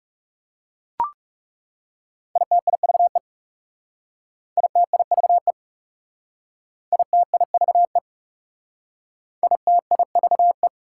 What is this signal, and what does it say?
Morse code, heard as a keyed tone, about 38 wpm, 700 Hz. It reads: ITIVE ITIVE ITIVE ITIVE
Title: Speed-Racing Prefixes and Suffixes 30wpm